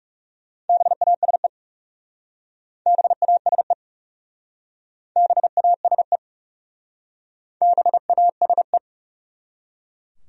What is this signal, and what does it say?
Morse code, heard as a keyed tone, about 37 wpm, 700 Hz. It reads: BASE BASE BASE BASE
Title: Speed-Racing Top 500 Words 30wpm